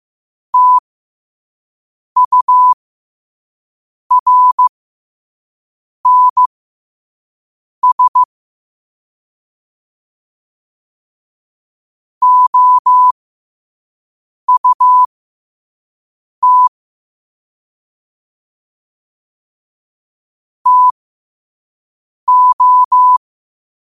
Morse code, heard as a keyed tone, about 15 wpm, spaced as for 5 wpm, 1000 Hz. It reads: TURNS OUT TO
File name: qotdfiles / 5wpm